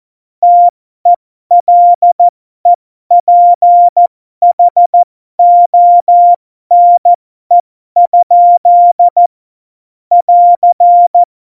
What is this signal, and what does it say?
Morse code, heard as a keyed tone, about 14 wpm, spaced as for 12 wpm, 700 Hz. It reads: TELEPHONE? <AR>
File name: Morsekurs / Witze